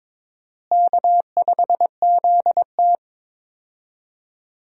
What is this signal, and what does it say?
Morse code, heard as a keyed tone, about 22 wpm, 700 Hz. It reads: K5ZT
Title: USA Call Signs - 4 Characters - Repeated 2x - 22wpm